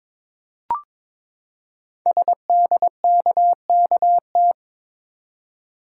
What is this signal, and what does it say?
Morse code, heard as a keyed tone, about 22 wpm, 700 Hz. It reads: SDKKT
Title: Code Groups - 5 Letters - 22wpm